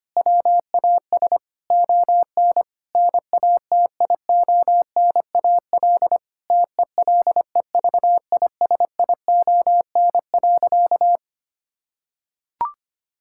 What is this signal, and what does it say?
Morse code, heard as a keyed tone, about 25 wpm, 700 Hz. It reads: WAS ON NATIONAL TELEVISION.
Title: Sentences from Top 1000 Words - Review Part 5 - 25wpm